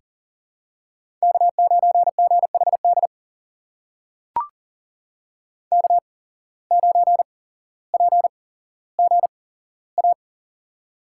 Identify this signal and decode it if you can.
Morse code, heard as a keyed tone, about 40 wpm, spaced as for 17 wpm, 700 Hz. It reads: K9GHD K9PGA
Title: USA Call Signs - 5 Characters - 8x Character Spacing - 40wpm